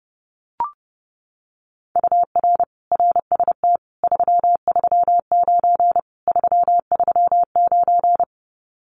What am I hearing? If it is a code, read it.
Morse code, heard as a keyed tone, about 32 wpm, 700 Hz. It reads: UR RST 339 339
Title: Common QSO Phrases Encore 30wpm